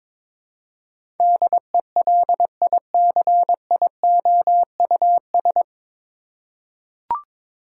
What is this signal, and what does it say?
Morse code, heard as a keyed tone, about 22 wpm, 700 Hz. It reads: DELICIOUS
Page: Morse Code Ninja practice 22wpm